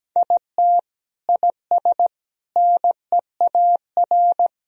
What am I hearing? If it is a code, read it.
Morse code, heard as a keyed tone, about 17 wpm, 700 Hz. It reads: IT IS NEAR
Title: Sentences from Top 1000 Words - Review Part 1 - 17wpm